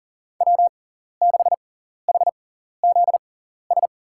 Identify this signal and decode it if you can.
Morse code, heard as a keyed tone, about 40 wpm, spaced as for 14 wpm, 700 Hz. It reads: W6HZS
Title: USA Call Signs - 5 Characters - 6x Character Spacing - 40wpm